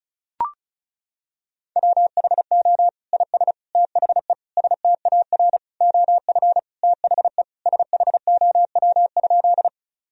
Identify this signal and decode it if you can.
Morse code, heard as a keyed tone, about 34 wpm, 700 Hz. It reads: WHO IS THE STAR OF THE SHOW?
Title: Sentences from Top 1000 Words - Review Part 2 - 35wpm